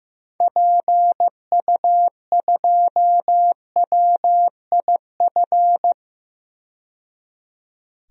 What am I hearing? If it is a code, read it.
Morse code, heard as a keyed tone, about 15 wpm, 700 Hz. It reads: PU2WIF
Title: DXCC-100 Call Signs - 6 Characters - 15wpm